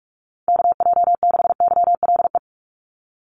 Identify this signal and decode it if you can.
Morse code, heard as a keyed tone, about 44 wpm, 700 Hz. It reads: KJ6YLE
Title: USA Call Signs 45wpm